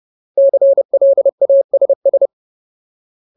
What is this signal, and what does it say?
Morse code, heard as a keyed tone, about 30 wpm, 550 Hz. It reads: CLASS
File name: Ultimate Word List / 30WPM Ultimate